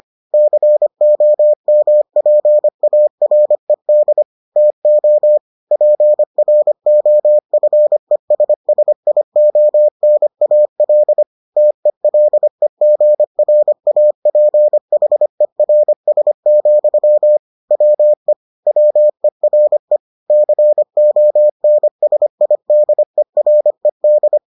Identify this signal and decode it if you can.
Morse code, heard as a keyed tone, about 25 wpm, 600 Hz. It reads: COMPARED TO PROFESSIONAL TELEGRAPHERS, WE WERE CONSIDERED